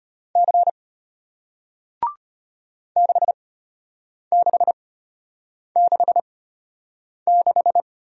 Morse code, heard as a keyed tone, about 33 wpm, spaced as for 8 wpm, 700 Hz. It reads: C 6666